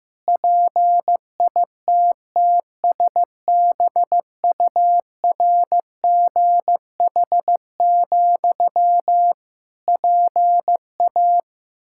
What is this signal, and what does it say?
Morse code, heard as a keyed tone, about 15 wpm, 700 Hz. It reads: PITTSBURGH, PA
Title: Practice QSO - Exchange 1 - Beginning - 15wpm